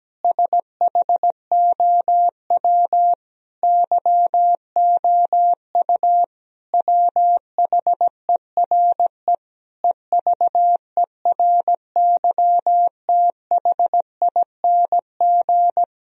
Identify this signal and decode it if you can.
Morse code, heard as a keyed tone, about 17 wpm, 700 Hz. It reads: SHOW YOU WHERE EVERYTHING
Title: Sentences from Top 1000 Words - Review Part 4 - 17wpm